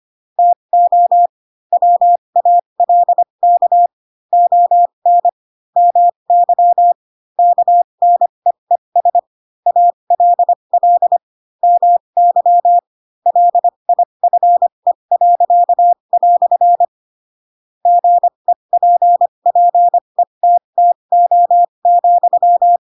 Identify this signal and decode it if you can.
Morse code, heard as a keyed tone, about 25 wpm, spaced as for 22 wpm, 700 Hz. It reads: TO WALK ON MY KNEES ALL MY LIFE." GEPPETTO,
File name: Pinocchio / Pinocchio ch7 18 25 ik1hge